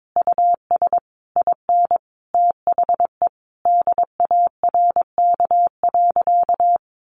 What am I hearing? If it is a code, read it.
Morse code, heard as a keyed tone, about 22 wpm, 700 Hz. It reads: US IN THE DARK.
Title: Sentences from Top 1400 Words 22wpm